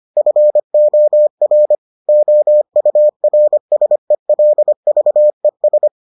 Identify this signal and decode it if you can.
Morse code, heard as a keyed tone, about 25 wpm, 600 Hz. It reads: FOR OURSELVES